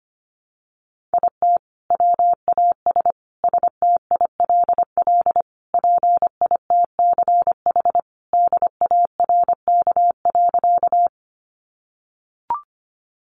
Morse code, heard as a keyed tone, about 25 wpm, 700 Hz. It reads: IT WAS STILL PITCH DARK.
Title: Sentences from Top 1300 Words 25wpm